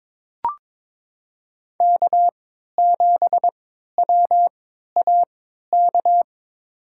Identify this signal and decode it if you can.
Morse code, heard as a keyed tone, about 22 wpm, spaced as for 12 wpm, 700 Hz. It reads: K7WAK
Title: USA Call Signs - 3x Character Spacing - 22wpm